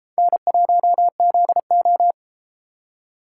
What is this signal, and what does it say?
Morse code, heard as a keyed tone, about 31 wpm, 700 Hz. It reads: N1ZO